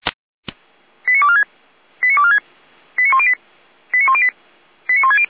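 Five-tone selective call (ZVEI), heard as quick runs of five tones, tones of about 70 ms. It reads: 89246 89246 89108 89108 89160
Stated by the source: Five-tone ZVEI selective call 89150 received by SDR.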